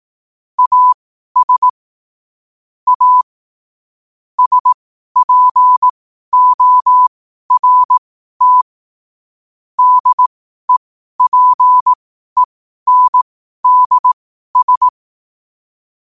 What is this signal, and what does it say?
Morse code, heard as a keyed tone, about 18 wpm, spaced as for 12 wpm, 1000 Hz. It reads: AS A SPORT DEPENDS
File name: qotdfiles / 12wpm